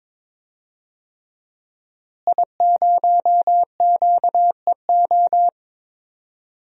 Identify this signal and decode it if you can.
Morse code, heard as a keyed tone, about 22 wpm, 700 Hz. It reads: I0QEO
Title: DXCC-100 Call Signs - 5 Characters - Repeated 3x - 22wpm